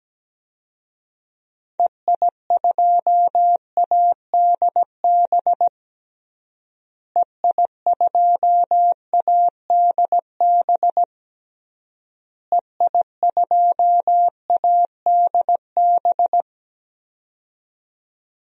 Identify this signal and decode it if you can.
Morse code, heard as a keyed tone, about 17 wpm, 700 Hz. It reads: EI2ADB EI2ADB EI2ADB
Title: DXCC-100 Call Signs - 6 Characters - Repeated 3x - 17wpm